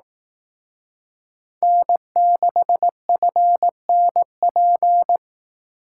Morse code, heard as a keyed tone, about 18 wpm, 700 Hz. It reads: N6FNP
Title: Speed Racing - USA 1x3 Call Signs 15wpm